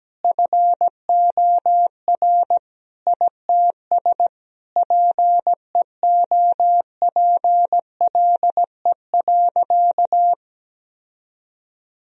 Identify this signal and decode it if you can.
Morse code, heard as a keyed tone, about 17 wpm, 700 Hz. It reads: FOR ITS PEOPLE.
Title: Sentences from Top 1000 Words - Review Part 3 - 17wpm